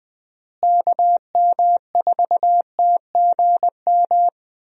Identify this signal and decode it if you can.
Morse code, heard as a keyed tone, about 21 wpm, 700 Hz. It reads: KM4TGM
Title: USA Call Signs - 6 Characters - Repeated 2x - 20wpm